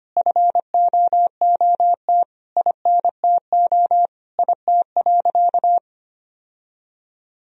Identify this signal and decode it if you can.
Morse code, heard as a keyed tone, about 25 wpm, 700 Hz. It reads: FOOT INTO IT.
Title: Sentences from Top 1000 Words - Review Part 3 - 25wpm